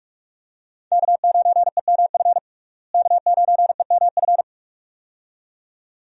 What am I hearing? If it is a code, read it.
Morse code, heard as a keyed tone, about 47 wpm, 700 Hz. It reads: K9EMF K9EMF
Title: USA Call Signs - 5 Characters - Repeated 2x - 45wpm